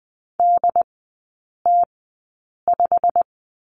Morse code, heard as a keyed tone, about 20 wpm, spaced as for 8 wpm, 700 Hz. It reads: DT5
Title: 100 Lesson 17e - Letter u - Three Characters 20wpm